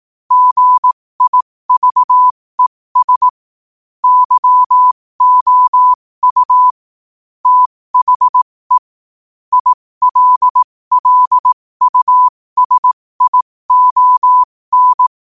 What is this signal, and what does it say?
Morse code, heard as a keyed tone, about 18 wpm, spaced as for 15 wpm, 1000 Hz. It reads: GIVES YOU THE ILLUSION
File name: qotdfiles / 15wpm